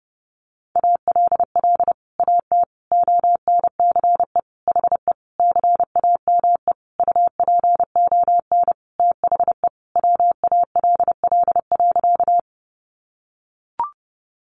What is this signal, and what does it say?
Morse code, heard as a keyed tone, about 30 wpm, 700 Hz. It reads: ALL AT ONCE HE CAME UPON THE WALL.
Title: Sentences from Top 1100 Words 30wpm